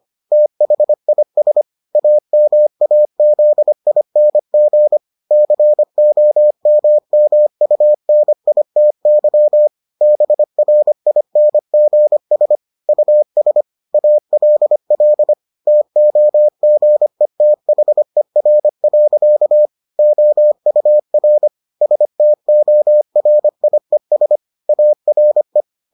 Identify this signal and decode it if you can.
Morse code, heard as a keyed tone, about 25 wpm, 600 Hz. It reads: THIS AMAZING COMMUNITY BRINGS US ALL TOGETHER. OUR STORIES ARE